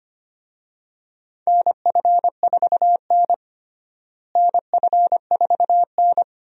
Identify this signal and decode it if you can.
Morse code, heard as a keyed tone, about 25 wpm, 700 Hz. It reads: NF4N NF4N
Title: USA Call Signs - 4 Characters - Repeated 2x - 25wpm